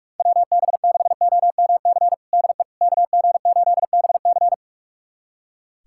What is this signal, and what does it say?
Morse code, heard as a keyed tone, about 44 wpm, 700 Hz. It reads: WB6OMC DE KG8BC
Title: USA Callsign DE Callsign 45wpm